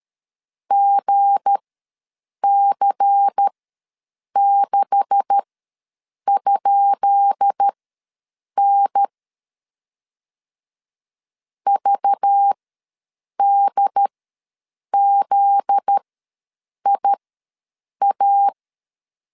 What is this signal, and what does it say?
Morse code, heard as a keyed tone, about 13 wpm, spaced as for 7 wpm, 800 Hz. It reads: GC6?N VDZIA
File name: LEZIONI / 10 SIGNS / 10D COMA ws 12wpm 3618